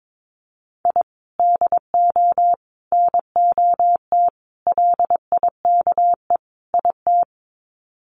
Morse code, heard as a keyed tone, about 22 wpm, 700 Hz. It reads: I DO NOT LIKE IT
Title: Sets of 5 Words Encore 22wpm